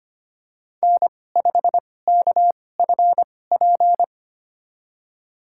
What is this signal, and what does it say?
Morse code, heard as a keyed tone, about 25 wpm, spaced as for 18 wpm, 700 Hz. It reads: N5KFP
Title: USA Call Signs - 2x Character Spacing - 25wpm